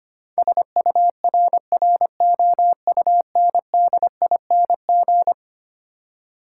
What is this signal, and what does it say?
Morse code, heard as a keyed tone, about 25 wpm, 700 Hz. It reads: SURROUNDING
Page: morse-code-ninja-practice-25wpm